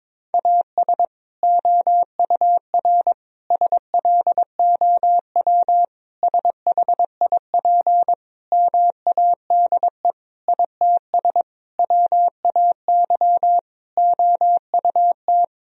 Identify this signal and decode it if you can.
Morse code, heard as a keyed tone, about 22 wpm, 700 Hz. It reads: AS OUR SLOW SHIP MADE ITS WAY OUT
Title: Sentences from Top 1000 Words - Review Part 2 - 22wpm